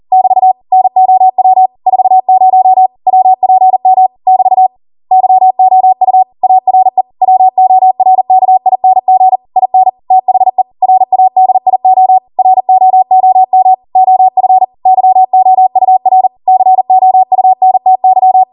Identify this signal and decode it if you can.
Morse code, heard as a keyed tone, about 40 wpm, 750 Hz. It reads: <BT> NOW 40 WPM <BT> YOU ARE WORKING IN THE RADIO ROOM OF YOUR COUNTY
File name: Morse / 40WPM